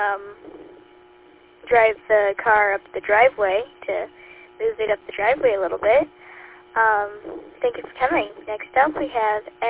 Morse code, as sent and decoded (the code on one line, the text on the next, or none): none